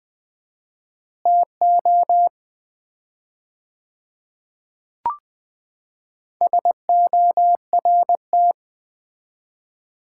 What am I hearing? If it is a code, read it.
Morse code, heard as a keyed tone, about 20 wpm, 700 Hz. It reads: TO SORT
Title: 4x Word-Spacing - Sets of 2 Words - 20wpm